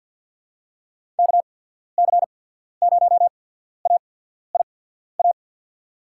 Morse code, heard as a keyed tone, about 48 wpm, spaced as for 14 wpm, 700 Hz. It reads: KC0AIA
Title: USA Call Signs - 6 Characters - 8x Character Spacing - 50wpm